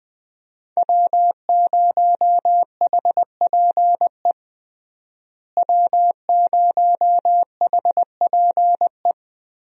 Morse code, heard as a keyed tone, about 21 wpm, 700 Hz. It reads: W0HPE W0HPE
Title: USA Call Signs - 5 Characters - Repeated 3x - 20wpm